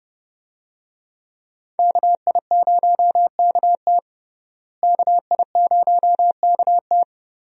Morse code, heard as a keyed tone, about 32 wpm, 700 Hz. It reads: KI0KT KI0KT
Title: USA Call Signs - 5 Characters - Repeated 3x - 30wpm